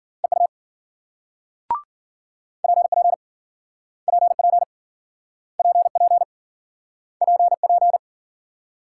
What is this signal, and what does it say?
Morse code, heard as a keyed tone, about 52 wpm, 700 Hz. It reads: EU PP PP PP PP